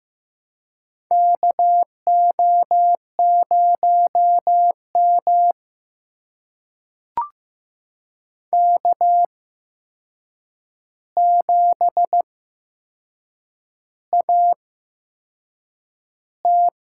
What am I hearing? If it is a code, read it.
Morse code, heard as a keyed tone, about 15 wpm, spaced as for 6 wpm, 700 Hz. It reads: KO0M K7AT
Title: USA Call Signs - 4 Characters - 8x Character Spacing - 15wpm